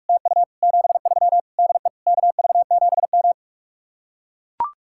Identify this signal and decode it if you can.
Morse code, heard as a keyed tone, about 45 wpm, 700 Hz. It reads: TU 73 DE KV7M